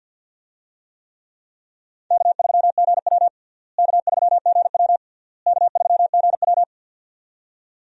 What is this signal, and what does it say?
Morse code, heard as a keyed tone, about 50 wpm, 700 Hz. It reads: K3GW K3GW K3GW
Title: USA Call Signs - 4 Characters - Repeated 3x - 50wpm